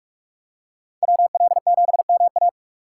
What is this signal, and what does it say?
Morse code, heard as a keyed tone, about 46 wpm, 700 Hz. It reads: WL7MA
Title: USA Call Signs - 5 Characters - 2x Character Spacing - 45wpm